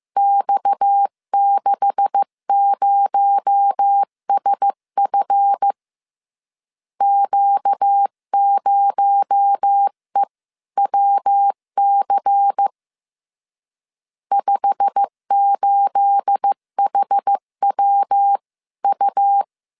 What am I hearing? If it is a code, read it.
Morse code, heard as a keyed tone, about 15 wpm, 800 Hz. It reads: X60SF Q0EWC 58HWU
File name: CW / DET / 1 9 1500C S B 14wpm 3302